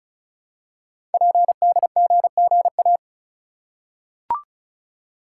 Morse code, heard as a keyed tone, about 36 wpm, 700 Hz. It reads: PDGGA
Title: Code Groups - 5 Letters - 35wpm